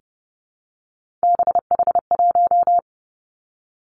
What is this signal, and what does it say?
Morse code, heard as a keyed tone, about 32 wpm, 700 Hz. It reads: BH1
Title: Three Challenging Letter-Number Combinations 30wpm